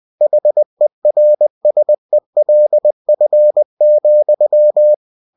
Morse code, heard as a keyed tone, about 20 wpm, 600 Hz. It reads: HERSELF,
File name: cw-books / winnie-the-pooh / winnie-20wpm-page-0071